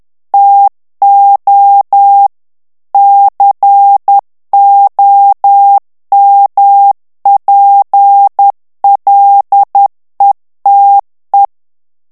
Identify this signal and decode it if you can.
Morse code, heard as a keyed tone, about 10 wpm, 800 Hz. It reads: TO COMPLETE
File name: ANGLAIS texte clair 3 vitesse 10 mots minute